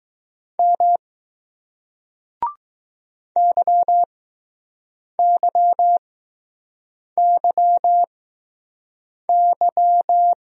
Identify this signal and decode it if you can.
Morse code, heard as a keyed tone, about 20 wpm, spaced as for 6 wpm, 700 Hz. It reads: M YYYY